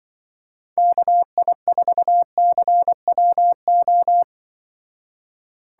Morse code, heard as a keyed tone, about 24 wpm, 700 Hz. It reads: KI4CWO
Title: Speed Racing - USA 2x3 Call Signs 20wpm